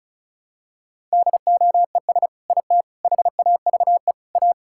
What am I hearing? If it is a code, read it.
Morse code, heard as a keyed tone, about 34 wpm, 700 Hz. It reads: DOES IT HAVE A
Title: Sentences from Top 1000 Words - Review Part 5 - 35wpm